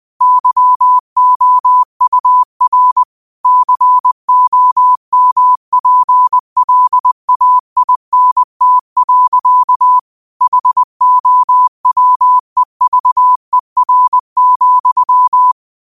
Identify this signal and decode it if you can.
Morse code, heard as a keyed tone, about 20 wpm, 1000 Hz. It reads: YOUR COMPLAINT. HOWEVER,
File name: qotdfiles / 20wpm